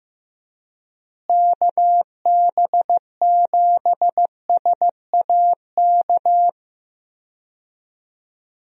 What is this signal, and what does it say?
Morse code, heard as a keyed tone, about 15 wpm, 700 Hz. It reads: KB7SAK
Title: USA Call Signs - 6 Characters - Repeated 2x - 15wpm